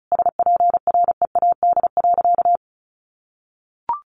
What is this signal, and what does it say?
Morse code, heard as a keyed tone, about 34 wpm, 700 Hz. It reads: SPREAD.
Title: Sentences from Top 1200 Words 35wpm